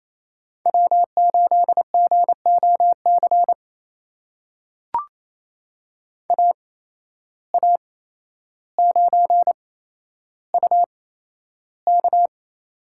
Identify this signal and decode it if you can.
Morse code, heard as a keyed tone, about 28 wpm, spaced as for 12 wpm, 700 Hz. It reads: W8GOC AA9UK
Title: USA Call Signs - 5 Characters - 8x Character Spacing - 28wpm